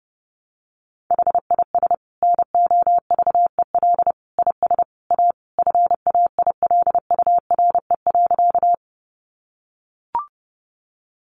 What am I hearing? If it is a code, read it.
Morse code, heard as a keyed tone, about 30 wpm, 700 Hz. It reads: HIS NOVEL IS A FAILURE.
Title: Sentences from Top 1300 Words 30wpm